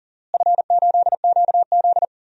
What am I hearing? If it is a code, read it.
Morse code, heard as a keyed tone, about 40 wpm, 700 Hz. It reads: F8QZ
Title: DXCC-100 Call Signs - 4 Characters - Repeated 2x - 40wpm